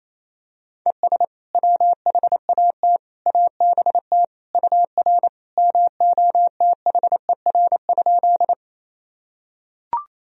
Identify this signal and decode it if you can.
Morse code, heard as a keyed tone, about 27 wpm, 700 Hz. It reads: ES WHAT ABT UR MOTHER?
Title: Sentences from Top 200 Words and Abbreviations 28wpm